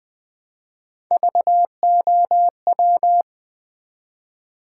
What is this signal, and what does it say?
Morse code, heard as a keyed tone, about 20 wpm, 700 Hz. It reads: VOW
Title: Build A Word - 1 Letter At A Time - 3-Letter Words - 20wpm